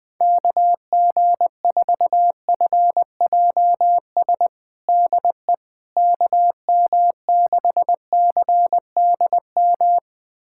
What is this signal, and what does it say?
Morse code, heard as a keyed tone, about 21 wpm, 700 Hz. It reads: KG4FJS DE KM6CDM
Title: USA Callsign DE Callsign 20wpm